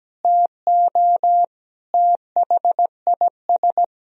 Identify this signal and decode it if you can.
Morse code, heard as a keyed tone, about 17 wpm, 700 Hz. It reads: TO THIS
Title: Sentences from Top 1000 Words - Review Part 4 - 17wpm